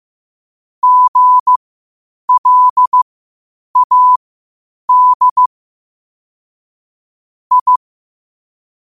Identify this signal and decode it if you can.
Morse code, heard as a keyed tone, about 15 wpm, spaced as for 8 wpm, 1000 Hz. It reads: GLAD I
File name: qotdfiles / 8wpm